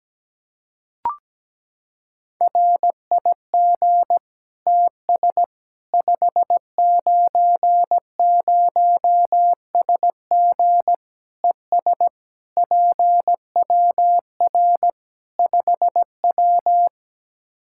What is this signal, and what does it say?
Morse code, heard as a keyed tone, about 17 wpm, 700 Hz. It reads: RIG TS 590SG ES PWR 5W
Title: Practice QSO - Exchange 2 - Info - 17wpm